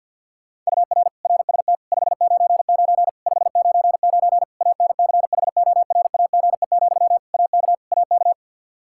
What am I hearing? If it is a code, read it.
Morse code, heard as a keyed tone, about 51 wpm, 700 Hz. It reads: UR RST 599 599 ANCHORAGE, AK AK
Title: Practice QSO - Exchange 1 - Beginning - 50wpm